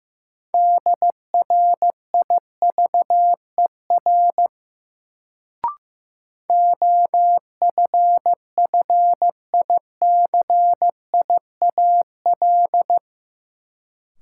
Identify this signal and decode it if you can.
Morse code, heard as a keyed tone, about 15 wpm, 700 Hz. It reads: DRIVER OFFICIAL
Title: Sentences from Top 1000 Words - New Word Drill - 15wpm